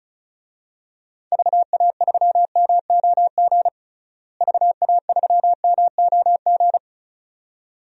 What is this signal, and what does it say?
Morse code, heard as a keyed tone, about 34 wpm, 700 Hz. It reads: VA3MOG VA3MOG
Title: DXCC-100 Call Signs - 6 Characters - Repeated 2x - 35wpm